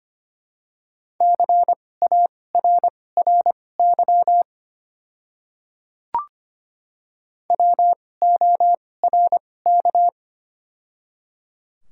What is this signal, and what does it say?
Morse code, heard as a keyed tone, about 25 wpm, spaced as for 18 wpm, 700 Hz. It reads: CARRY WORK
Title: 2x Character Spacing - Top 500 Words - 25wpm